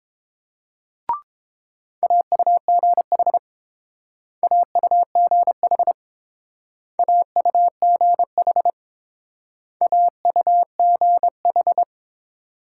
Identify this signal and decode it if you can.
Morse code, heard as a keyed tone, about 28 wpm, 700 Hz. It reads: AUGH AUGH AUGH AUGH